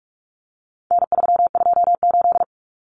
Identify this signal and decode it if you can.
Morse code, heard as a keyed tone, about 44 wpm, 700 Hz. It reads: N328
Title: Common Cut Numbers - 4-Digit 45wpm